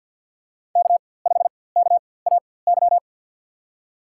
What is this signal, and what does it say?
Morse code, heard as a keyed tone, about 48 wpm, spaced as for 23 wpm, 700 Hz. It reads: K5KAY
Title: USA Call Signs - 5 Characters - 4x Character Spacing - 50wpm